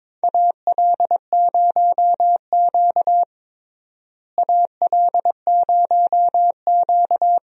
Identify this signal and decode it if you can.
Morse code, heard as a keyed tone, about 22 wpm, 700 Hz. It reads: AL0Q AL0Q
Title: USA Call Signs - 4 Characters - Repeated 3x - 22wpm